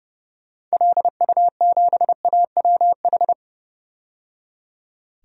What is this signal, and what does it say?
Morse code, heard as a keyed tone, about 30 wpm, 700 Hz. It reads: LU7AWH
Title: DXCC-100 Call Signs - 6 Characters - 30wpm